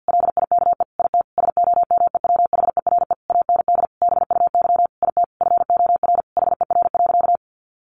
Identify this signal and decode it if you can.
Morse code, heard as a keyed tone, about 50 wpm, 700 Hz. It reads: LIKE IT SOMEWHERE AND BUY IT FOR HER.
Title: Sentences from Top 1400 Words 50wpm